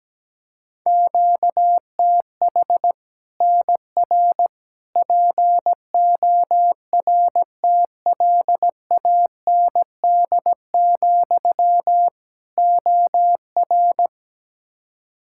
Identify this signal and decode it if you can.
Morse code, heard as a keyed tone, about 17 wpm, 700 Hz. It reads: QTH NR PORTLAND, OR
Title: Practice QSO - Exchange 1 - Beginning - 17wpm